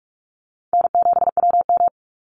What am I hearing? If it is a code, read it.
Morse code, heard as a keyed tone, about 44 wpm, 690 Hz. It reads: N7WM
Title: NA SOTA Chaser Callsigns 45wpm